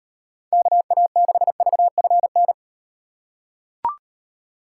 Morse code, heard as a keyed tone, about 38 wpm, 700 Hz. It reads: KA6VFN